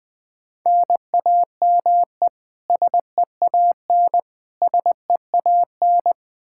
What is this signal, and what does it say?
Morse code, heard as a keyed tone, about 20 wpm, 700 Hz. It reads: NAME SEAN SEAN
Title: Practice QSO - Exchange 1 - Beginning - 20wpm